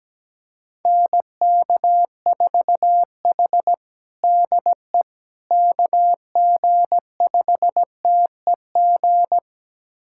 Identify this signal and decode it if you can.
Morse code, heard as a keyed tone, about 17 wpm, 700 Hz. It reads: NK4H DE KG5TEG
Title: USA Callsign DE Callsign 17wpm